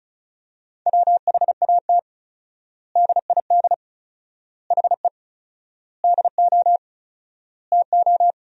Morse code, heard as a keyed tone, about 34 wpm, 700 Hz. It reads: WHAT DID HE DO TO
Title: 4x Word-Spacing - Sentences from Top 100 Words - 35wpm